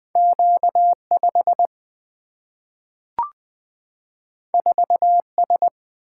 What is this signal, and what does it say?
Morse code, heard as a keyed tone, about 20 wpm, 700 Hz. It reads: Q5 4S
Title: Single-Letter Single-Number 20wpm